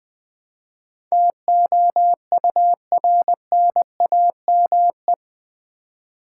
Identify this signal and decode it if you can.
Morse code, heard as a keyed tone, about 21 wpm, 700 Hz. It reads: TOURNAME
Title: Build A Word - 4 Letters At A Time - 10-Letter Words - 20wpm